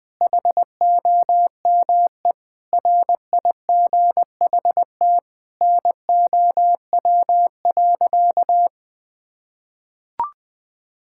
Morse code, heard as a keyed tone, about 21 wpm, 700 Hz. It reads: HOME RIGHT NOW.
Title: Sentences with 5-Word Phrases - 20wpm